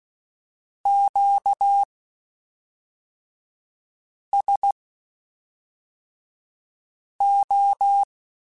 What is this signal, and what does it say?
Morse code, heard as a keyed tone, about 16 wpm, spaced as for 3 wpm, 800 Hz. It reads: QSO
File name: Q Calls / Q Calls-mp3s / Q Calls SPSL2 @5